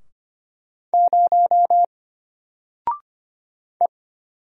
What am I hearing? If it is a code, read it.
Morse code, heard as a keyed tone, about 25 wpm, 700 Hz. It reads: 0 E
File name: Single-Letter-Number-25wpm